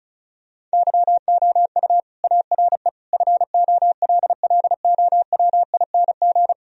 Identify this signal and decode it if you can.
Morse code, heard as a keyed tone, about 34 wpm, 700 Hz. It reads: YOU ARE FOLLOWING
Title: Sentences from Top 1000 Words - Review Part 4 - 35wpm